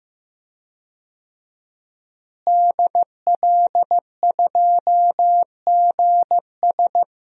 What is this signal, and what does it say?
Morse code, heard as a keyed tone, about 15 wpm, 700 Hz. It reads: DL2GS
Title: DXCC-100 Call Signs - 5 Characters - Repeated 3x - 15wpm